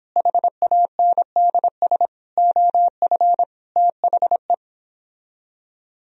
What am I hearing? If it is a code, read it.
Morse code, heard as a keyed tone, about 29 wpm, 700 Hz. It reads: HANDS OF THE